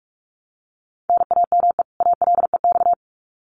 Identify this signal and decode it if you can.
Morse code, heard as a keyed tone, about 44 wpm, 700 Hz. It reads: NAME ALEX
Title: Common QSO Phrases 45wpm